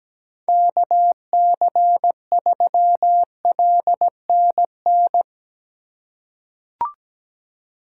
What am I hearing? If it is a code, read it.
Morse code, heard as a keyed tone, about 17 wpm, 700 Hz. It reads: KC3LNN
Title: USA Call Signs - 6 Characters - 3x Character Spacing - 17wpm